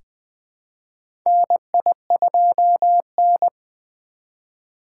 Morse code, heard as a keyed tone, about 20 wpm, 700 Hz. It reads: NI2N
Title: USA Call Signs - 4 Characters - 2x Character Spacing - 20wpm